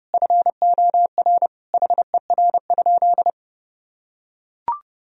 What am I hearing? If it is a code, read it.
Morse code, heard as a keyed tone, about 30 wpm, 700 Hz. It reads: FOR HER?
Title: Sentences from Top 1000 Words - Review Part 3 - 30wpm